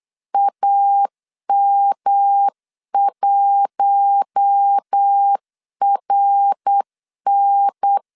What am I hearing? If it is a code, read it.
Morse code, heard as a keyed tone, about 8 wpm, 800 Hz. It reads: AM1RN
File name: LEZIONI / 7 GF9Z / 7 GF 8wpm 3108